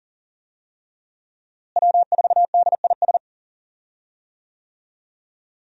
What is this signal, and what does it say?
Morse code, heard as a keyed tone, about 37 wpm, 700 Hz. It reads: W4DIS
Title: USA Call Signs - 5 Characters - Repeated 3x - 40wpm